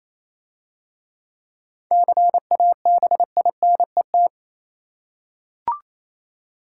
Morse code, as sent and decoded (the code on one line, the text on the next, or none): -.-. .- -... .. -. -
CABINET